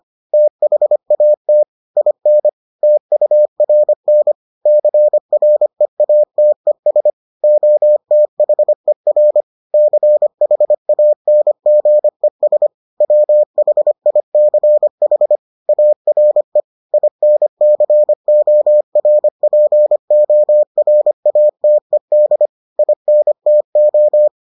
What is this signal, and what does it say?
Morse code, heard as a keyed tone, about 25 wpm, 600 Hz. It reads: THAT IN TURN CREATES OTHER CHANGES WHICH ARE INCORPORATED INTO